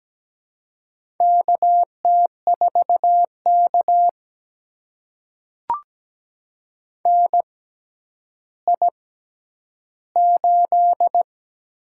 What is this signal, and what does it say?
Morse code, heard as a keyed tone, about 17 wpm, spaced as for 10 wpm, 700 Hz. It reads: KT4K NI8
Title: USA Call Signs - 4 Characters - 6x Character Spacing - 17wpm